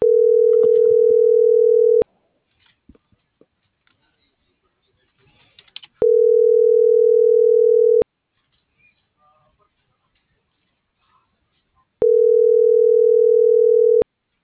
Ringback tone (North America), 440 + 480 Hz.